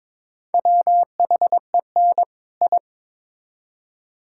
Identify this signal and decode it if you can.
Morse code, heard as a keyed tone, about 22 wpm, 700 Hz. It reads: WHEN I
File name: Build A Phrase - 2-Word Phrases - 22wpm